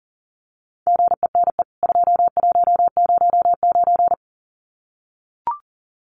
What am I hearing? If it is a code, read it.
Morse code, heard as a keyed tone, about 37 wpm, 700 Hz. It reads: GENE 2109
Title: CWT Exchange 40wpm